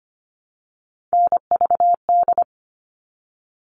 Morse code, heard as a keyed tone, about 25 wpm, 700 Hz. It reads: NVD
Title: All Cut Numbers - 3-Digit 25wpm